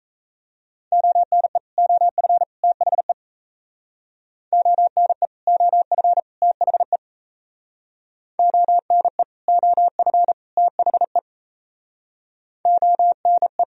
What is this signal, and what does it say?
Morse code, heard as a keyed tone, about 35 wpm, 700 Hz. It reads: ONE OF THE ONE OF THE ONE OF THE ONE
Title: Speed-Racing - Sets of 3 Words 28wpm